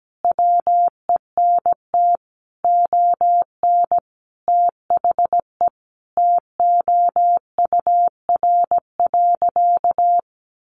Morse code, heard as a keyed tone, about 17 wpm, 700 Hz. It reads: WENT ON THE TOUR.
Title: Sentences from Top 1400 Words 17wpm